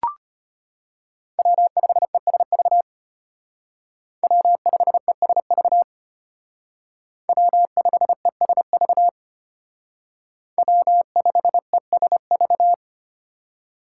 Morse code, heard as a keyed tone, about 32 wpm, 700 Hz. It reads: W5ESV W5ESV W5ESV W5ESV